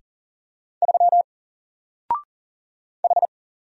Morse code, heard as a keyed tone, about 39 wpm, 700 Hz. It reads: 3 H
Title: Single Character (Letter:Number:Punctuation:Prosign) 40wpm